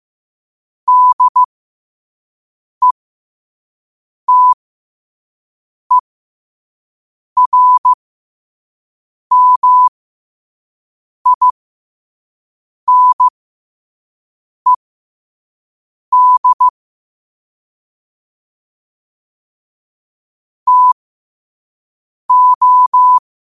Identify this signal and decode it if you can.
Morse code, heard as a keyed tone, about 15 wpm, spaced as for 5 wpm, 1000 Hz. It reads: DETERMINED TO
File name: qotdfiles / 5wpm